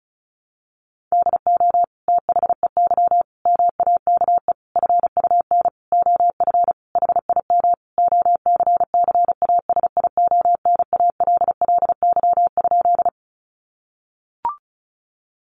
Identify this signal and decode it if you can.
Morse code, heard as a keyed tone, about 34 wpm, 700 Hz. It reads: DO THEY MAKE FUN OF HIM OCCASIONALLY?
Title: Sentences from Top 1300 Words 35wpm